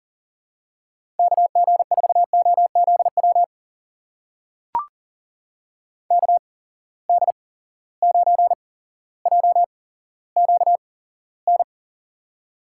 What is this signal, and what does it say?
Morse code, heard as a keyed tone, about 40 wpm, spaced as for 17 wpm, 700 Hz. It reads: KG4OZW KD9JQN
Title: USA Call Signs - 8x Character Spacing - 40wpm